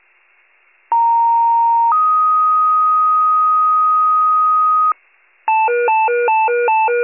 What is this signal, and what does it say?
Two-tone page, heard as a one-second tone, then a three-second tone, about 930 then 1250 Hz. It is heard from 0.9 to 4.9 s.
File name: pager / 82 FIRE / wcalert